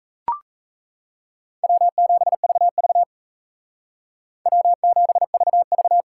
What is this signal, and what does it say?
Morse code, heard as a keyed tone, about 35 wpm, 700 Hz. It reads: W7VV W7VV
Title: Speed Racing - USA 1x2 Call Signs 28wpm